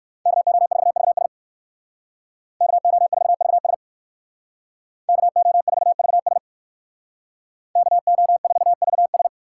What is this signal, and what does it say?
Morse code, heard as a keyed tone, about 56 wpm, 700 Hz. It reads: KO4VS KO4VS KO4VS KO4VS
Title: Speed Racing - USA Call Signs 45wpm